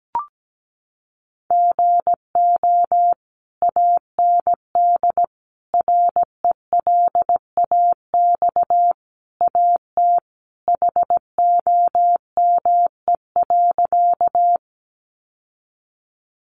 Morse code, heard as a keyed tone, about 17 wpm, 700 Hz. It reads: GO AND RELAX AT HOME.
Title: Sentences from Top 1400 Words 17wpm